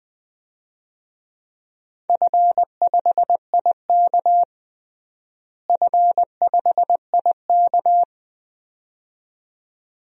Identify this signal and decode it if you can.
Morse code, heard as a keyed tone, about 21 wpm, 700 Hz. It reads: F5IK F5IK
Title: DXCC-100 Call Signs - 4 Characters - Repeated 2x - 20wpm